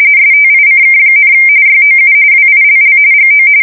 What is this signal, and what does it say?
RTTY, about 45.45 baud. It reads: STICK, YOU WILL GO
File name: fsk170 / usnavy3